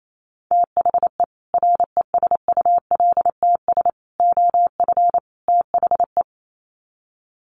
Morse code, heard as a keyed tone, about 28 wpm, 700 Hz. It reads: THE RESULTS OF THE
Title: Sets of 4 Words 28wpm